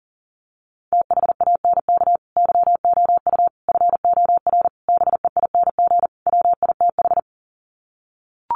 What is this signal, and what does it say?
Morse code, heard as a keyed tone, about 40 wpm, 700 Hz. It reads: THANK YOU FOR BEING WITH
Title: Sets of 5 Words 40wpm